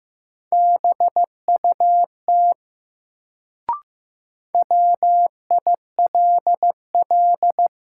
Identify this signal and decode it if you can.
Morse code, heard as a keyed tone, about 15 wpm, 700 Hz. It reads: BUT WILL
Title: Sentences from Top 100 Words - New Word Drill - 15wpm